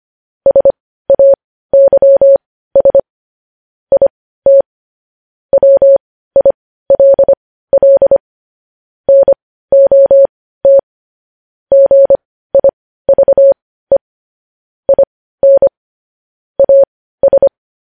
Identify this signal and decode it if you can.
Morse code, heard as a keyed tone, about 25 wpm, spaced as for 15 wpm, 550 Hz. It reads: SAYS IT WILL NOT GIVE IN AS
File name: podcast / News headlines-1x-25x15